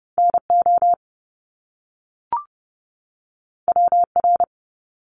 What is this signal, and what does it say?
Morse code, heard as a keyed tone, about 30 wpm, 700 Hz. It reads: NO WR
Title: Top Two Letter Combinations in Words 30wpm